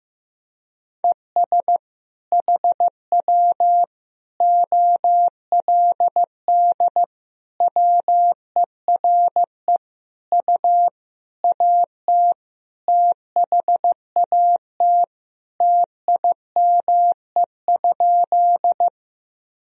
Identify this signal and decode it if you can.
Morse code, heard as a keyed tone, about 15 wpm, 700 Hz. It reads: ES HW OLD WERE U AT THAT TIME?
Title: Sentences from Top 200 Words and Abbreviations 15wpm